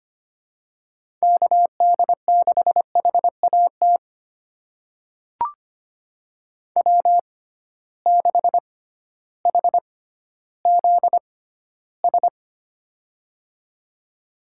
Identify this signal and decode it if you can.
Morse code, heard as a keyed tone, about 25 wpm, spaced as for 14 wpm, 700 Hz. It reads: KD6HAT W6HZS
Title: USA Call Signs - 6x Character Spacing - 25wpm